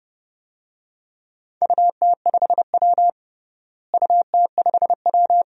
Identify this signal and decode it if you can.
Morse code, heard as a keyed tone, about 30 wpm, 700 Hz. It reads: UT5W UT5W
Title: DXCC-100 Call Signs - 4 Characters - Repeated 2x - 30wpm